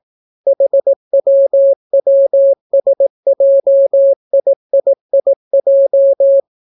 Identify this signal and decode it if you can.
Morse code, heard as a keyed tone, about 18 wpm, 550 Hz. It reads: HWWSJIIIJ